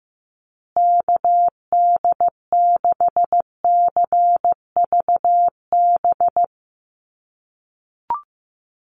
Morse code, heard as a keyed tone, about 15 wpm, 700 Hz. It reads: KD6CVB
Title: USA Call Signs Repeated 3x 15wpm